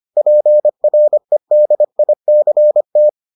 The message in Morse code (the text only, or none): PREDICT